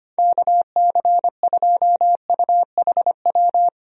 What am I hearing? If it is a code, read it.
Morse code, heard as a keyed tone, about 25 wpm, 700 Hz. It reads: KC2UHW
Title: USA Call Signs - 6 Characters - 25wpm